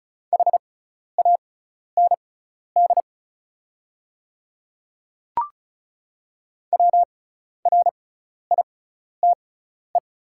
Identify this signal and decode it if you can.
Morse code, heard as a keyed tone, about 34 wpm, spaced as for 12 wpm, 700 Hz. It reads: HAND WRITE
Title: 6x Character Spacing - Top 300 Words - 35wpm